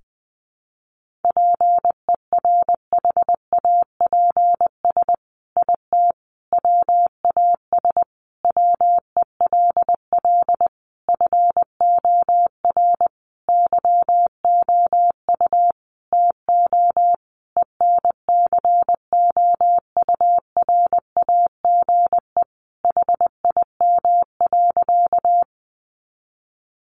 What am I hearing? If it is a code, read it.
Morse code, heard as a keyed tone, about 20 wpm, 700 Hz. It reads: PERHAPS IT WAS WELL FOR YOU TO ENCOURAGE HIM.
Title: Sentences from Top 1400 Words 20wpm